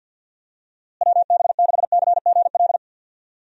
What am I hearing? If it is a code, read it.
Morse code, heard as a keyed tone, about 49 wpm, 700 Hz. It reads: WB6CGL
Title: USA Call Signs - 6 Characters - Repeated 2x - 50wpm